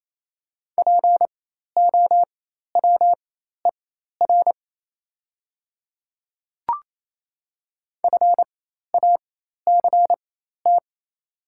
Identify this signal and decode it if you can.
Morse code, heard as a keyed tone, about 27 wpm, spaced as for 13 wpm, 700 Hz. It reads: POWER FACT
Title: 4x Character Spacing - Top 500 Words - 28wpm